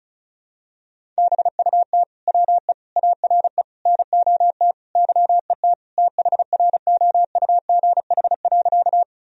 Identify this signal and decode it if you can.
Morse code, heard as a keyed tone, about 34 wpm, 700 Hz. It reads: BUT WE ARE NOT YET THROUGH.
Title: Sentences from Top 1000 Words - Review Part 1 - 35wpm